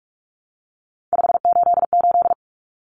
Morse code, heard as a keyed tone, about 44 wpm, 700 Hz. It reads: 588
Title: Numbers - 3-Digit 45wpm